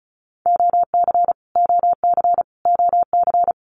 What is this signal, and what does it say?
Morse code, heard as a keyed tone, about 34 wpm, 700 Hz. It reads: OC OC OC
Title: Abbreviations Repeated 3x - 35wpm